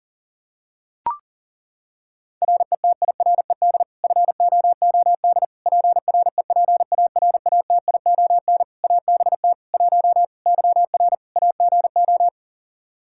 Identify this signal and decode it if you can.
Morse code, heard as a keyed tone, about 40 wpm, 700 Hz. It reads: RETIRED FOOD PREPARATION ABT 1 YR AGO
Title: Practice QSO - Exchange 3 - More Info - 40wpm